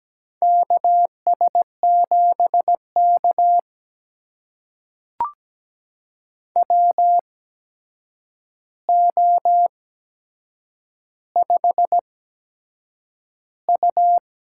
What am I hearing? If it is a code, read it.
Morse code, heard as a keyed tone, about 17 wpm, spaced as for 7 wpm, 700 Hz. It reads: KS7K WO5U
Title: USA Call Signs - 4 Characters - 8x Character Spacing - 17wpm